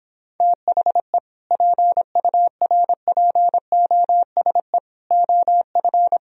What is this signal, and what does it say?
Morse code, heard as a keyed tone, about 29 wpm, 700 Hz. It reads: THE PURPOSE OF